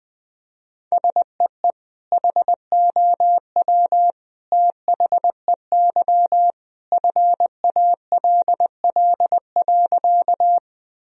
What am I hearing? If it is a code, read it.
Morse code, heard as a keyed tone, about 20 wpm, 700 Hz. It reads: SEE HOW THEY FALL.
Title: Sentences from Top 1000 Words - Review Part 2 - 20wpm